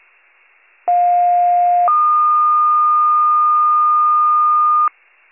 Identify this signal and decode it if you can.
Two-tone page, heard as a one-second tone, then a three-second tone, about 710 then 1220 Hz. It is heard from 0.9 to 4.9 s.